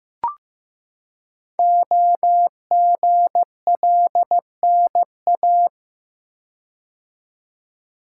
Morse code, heard as a keyed tone, about 15 wpm, 700 Hz. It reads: OGLNA
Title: Code Groups - 5 Letters - 15wpm